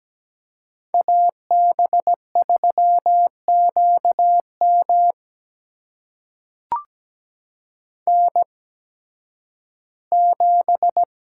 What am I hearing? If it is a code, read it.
Morse code, heard as a keyed tone, about 17 wpm, spaced as for 11 wpm, 700 Hz. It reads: AB3QM N7
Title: USA Call Signs - 8x Character Spacing - 17wpm